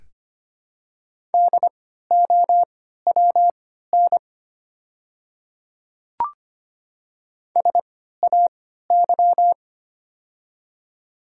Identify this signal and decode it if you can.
Morse code, heard as a keyed tone, about 25 wpm, spaced as for 14 wpm, 700 Hz. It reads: DOWN SAY
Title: 3x Character Spacing - Top 300 Words - 25wpm